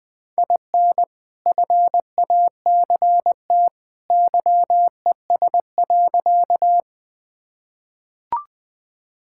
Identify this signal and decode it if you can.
Morse code, heard as a keyed tone, about 20 wpm, 700 Hz. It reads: IN FACT YES.
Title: Sentences with 2-Word Phrases - 20wpm